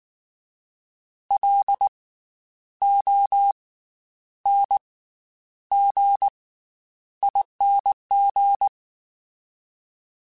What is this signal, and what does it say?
Morse code, heard as a keyed tone, about 19 wpm, spaced as for 9 wpm, 800 Hz. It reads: LONGING
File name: Special INGs / Special INGs-mp3s / Slow ing @18